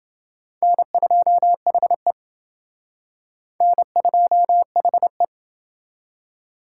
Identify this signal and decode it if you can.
Morse code, heard as a keyed tone, about 29 wpm, 700 Hz. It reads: N2HE N2HE